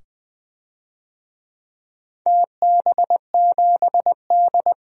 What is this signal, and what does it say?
Morse code, heard as a keyed tone, about 20 wpm, 700 Hz. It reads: TB7D
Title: DXCC-100 Call Signs - 4 Characters - 20wpm